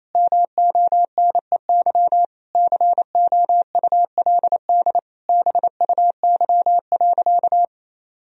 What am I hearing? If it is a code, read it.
Morse code, heard as a keyed tone, about 28 wpm, 700 Hz. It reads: MONEY COULD BUY.
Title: Sentences from Top 1000 Words - Review Part 4 - 28wpm